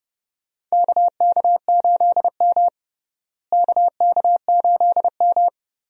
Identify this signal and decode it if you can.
Morse code, heard as a keyed tone, about 30 wpm, 700 Hz. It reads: KK8M KK8M
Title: USA Call Signs - 4 Characters - Repeated 3x - 30wpm